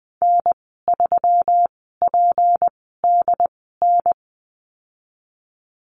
Morse code, heard as a keyed tone, about 20 wpm, spaced as for 15 wpm, 700 Hz. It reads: N3PDN
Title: 167 Lesson 27g - Number 7 - Call Signs 20wpm